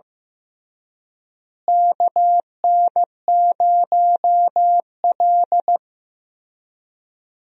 Morse code, heard as a keyed tone, about 15 wpm, 700 Hz. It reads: KN0L
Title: USA Call Signs - 4 Characters - Repeated 3x - 15wpm